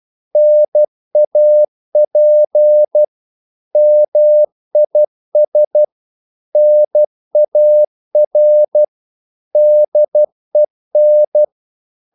Morse code, heard as a keyed tone, about 12 wpm, 600 Hz. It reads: NAP MIS NAR DEN
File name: Korte ord DK 12wpm